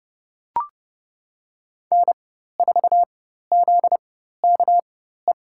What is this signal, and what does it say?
Morse code, heard as a keyed tone, about 30 wpm, spaced as for 14 wpm, 700 Hz. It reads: N4ZKE
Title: USA Call Signs - 5 Characters - 4x Character Spacing - 30wpm